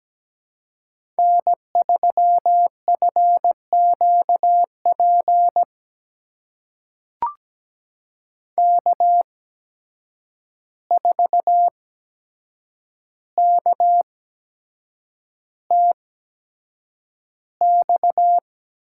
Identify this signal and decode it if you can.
Morse code, heard as a keyed tone, about 17 wpm, spaced as for 7 wpm, 700 Hz. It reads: N3FQP K4KTX
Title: USA Call Signs - 5 Characters - 8x Character Spacing - 17wpm